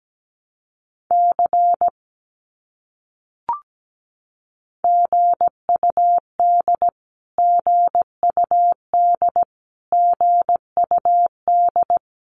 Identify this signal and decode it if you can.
Morse code, heard as a keyed tone, about 17 wpm, 700 Hz. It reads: C GUD GUD GUD
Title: Abbreviations Repeated 3x - 17wpm